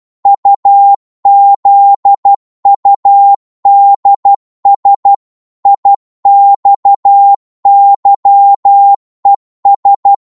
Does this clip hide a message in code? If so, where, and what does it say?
Yes: Morse code from 0.3 to 10.1 s, heard as a keyed tone, about 12 wpm, 800 Hz. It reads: UZUDS IXYES